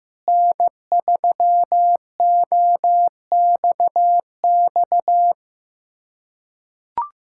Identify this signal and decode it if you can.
Morse code, heard as a keyed tone, about 15 wpm, 700 Hz. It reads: N3OXX
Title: USA Callsign DE Callsign 15wpm